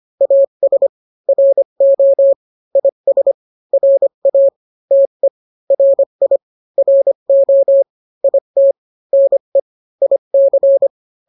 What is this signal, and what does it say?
Morse code, heard as a keyed tone, about 25 wpm, 550 Hz. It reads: AS RO IS RA TE RI RO IT NE IC